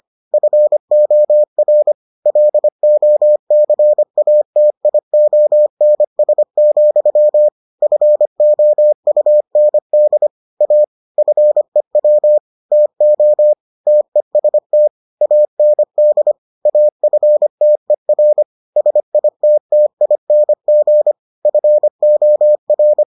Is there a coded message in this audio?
Yes: Morse code from 0.3 to 23.0 s, heard as a keyed tone, about 25 wpm, 600 Hz. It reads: FOR LOCATIONS, FOUND A FEW TO TEST AND AFTER SITTING FOR